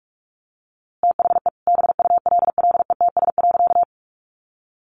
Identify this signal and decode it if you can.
Morse code, heard as a keyed tone, about 46 wpm, 700 Hz. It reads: THE BULLETS.
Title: A Sound Of Thunder 45wpm